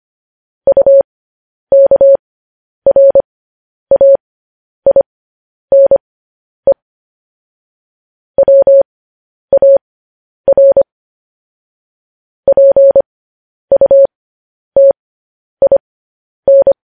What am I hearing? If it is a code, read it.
Morse code, heard as a keyed tone, about 25 wpm, spaced as for 10 wpm, 550 Hz. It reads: UKRAINE WAR PUTIN
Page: News Headlines in Morse Code at 10 WPM